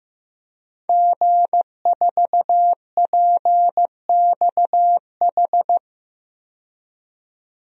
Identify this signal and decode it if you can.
Morse code, heard as a keyed tone, about 15 wpm, 700 Hz. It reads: G4PXH
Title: DXCC-100 Call Signs - 5 Characters - 15wpm